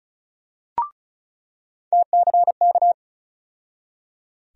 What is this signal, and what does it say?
Morse code, heard as a keyed tone, about 36 wpm, 700 Hz. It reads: TCK
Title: USA Call Sign Postfixes - 3 Letters 35wpm